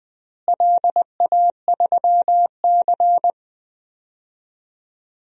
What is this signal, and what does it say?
Morse code, heard as a keyed tone, about 20 wpm, 700 Hz. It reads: LA3C
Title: DXCC-100 Call Signs - 4 Characters - Repeated 3x - 20wpm